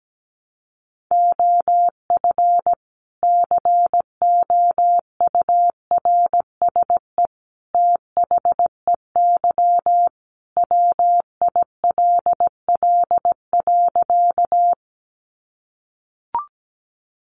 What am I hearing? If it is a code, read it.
Morse code, heard as a keyed tone, about 17 wpm, 700 Hz. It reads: OF COURSE THEY WILL.
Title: Sentences from Top 500 Words 17wpm